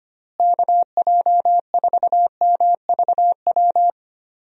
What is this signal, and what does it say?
Morse code, heard as a keyed tone, about 25 wpm, 700 Hz. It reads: KJ4MVW
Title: USA Call Signs - 6 Characters - Repeated 3x - 25wpm